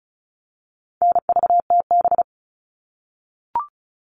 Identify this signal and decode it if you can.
Morse code, heard as a keyed tone, about 34 wpm, 700 Hz. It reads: NVTB
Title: All Cut Numbers - 4-Digit 35wpm